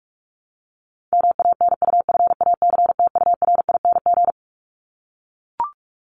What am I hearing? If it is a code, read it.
Morse code, heard as a keyed tone, about 44 wpm, 700 Hz. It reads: MANUFACTURING
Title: ING Suffix - Top 100 Words with Base Words 45wpm